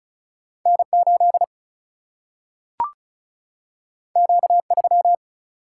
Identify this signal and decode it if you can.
Morse code, heard as a keyed tone, about 36 wpm, 700 Hz. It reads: N8 Q3
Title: Single-Letter Single-Number 35wpm